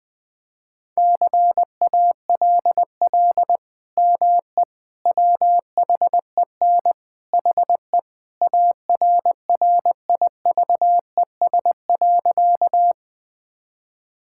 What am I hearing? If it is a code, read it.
Morse code, heard as a keyed tone, about 20 wpm, 700 Hz. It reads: CALL ME WHEN HE ARRIVES.
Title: Sentences with 2-Word Phrases - 20wpm